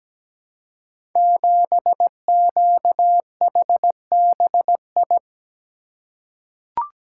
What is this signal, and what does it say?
Morse code, heard as a keyed tone, about 17 wpm, 700 Hz. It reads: 7QHBI
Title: Code Groups - 5 Letter:Numbers - 17wpm